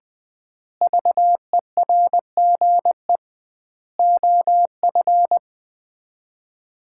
Morse code, heard as a keyed tone, about 20 wpm, 700 Hz. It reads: VERGE OF
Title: 2x Word-Spacing - Sets of 4 Words - 20wpm